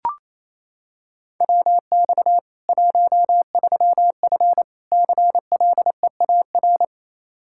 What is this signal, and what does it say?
Morse code, heard as a keyed tone, about 28 wpm, 700 Hz. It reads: WX 13F CLEAR
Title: K1USN SST QSO Practice 28wpm